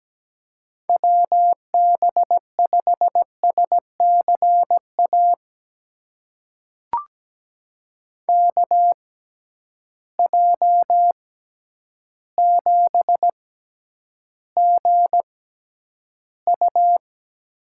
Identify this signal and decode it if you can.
Morse code, heard as a keyed tone, about 17 wpm, spaced as for 9 wpm, 700 Hz. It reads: WB5SCA KJ7GU
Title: USA Call Signs - 6 Characters - 6x Character Spacing - 17wpm